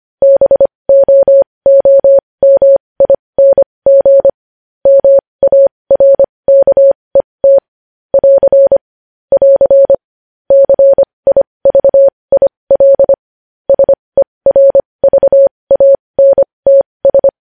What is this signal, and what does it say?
Morse code, heard as a keyed tone, about 25 wpm, spaced as for 20 wpm, 550 Hz. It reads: BOOMING MARKET <AR> <AR> CIVIL SERVANTS
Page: News Headlines in Morse Code at 20 WPM